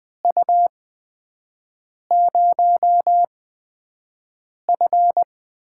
Morse code, heard as a keyed tone, about 21 wpm, spaced as for 5 wpm, 700 Hz. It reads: U0F
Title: USA Call Signs - 4 Characters - 8x Character Spacing - 20wpm